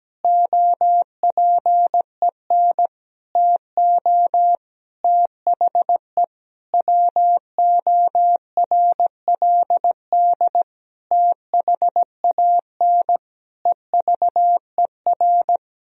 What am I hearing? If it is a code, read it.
Morse code, heard as a keyed tone, about 17 wpm, 700 Hz. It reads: OPEN TO THE WORLD THAN EVER
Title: Sentences from Top 1000 Words - Review Part 3 - 17wpm